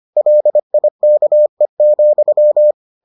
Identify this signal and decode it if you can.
Morse code, heard as a keyed tone, about 25 wpm, 600 Hz. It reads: LIKE,